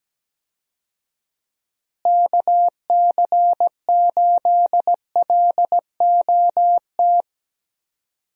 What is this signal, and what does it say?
Morse code, heard as a keyed tone, about 17 wpm, 700 Hz. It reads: KC8LOT
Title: USA Call Signs - 6 Characters - Repeated 2x - 17wpm